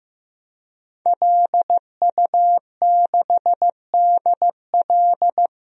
Morse code, heard as a keyed tone, about 15 wpm, 700 Hz. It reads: LU6DL
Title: DXCC-100 Call Signs - 5 Characters - 15wpm